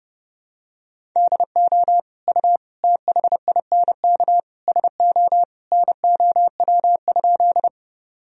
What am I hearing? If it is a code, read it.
Morse code, heard as a keyed tone, about 30 wpm, 700 Hz. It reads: DO U THINK SO NOW?
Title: Sentences from Top 200 Words and Abbreviations 30wpm